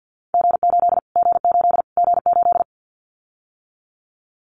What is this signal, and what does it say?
Morse code, heard as a keyed tone, about 50 wpm, 700 Hz. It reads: G8 G8 G8
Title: Callsign Prefixes for Top 25 DXCC - Repeated 3x - 50wpm